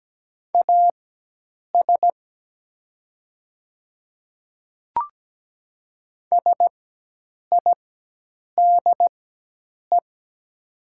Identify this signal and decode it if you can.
Morse code, heard as a keyed tone, about 17 wpm, spaced as for 8 wpm, 700 Hz. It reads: AS SIDE
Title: 4x Character Spacing - Top 300 Words - 17wpm